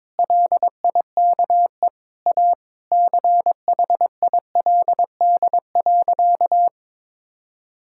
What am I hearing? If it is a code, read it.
Morse code, heard as a keyed tone, about 22 wpm, 700 Hz. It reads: LIKE A CHILD.
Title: Sentences from Top 1000 Words - Review Part 6 - 22wpm